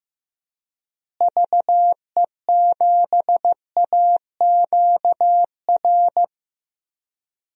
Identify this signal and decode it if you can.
Morse code, heard as a keyed tone, about 15 wpm, 700 Hz. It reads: VE7AQR
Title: DXCC-100 Call Signs - 6 Characters - Repeated 2x - 15wpm